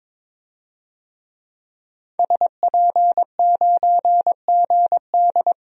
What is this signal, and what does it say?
Morse code, heard as a keyed tone, about 22 wpm, 700 Hz. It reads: SP9GD
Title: DXCC-100 Call Signs - 5 Characters - Repeated 3x - 22wpm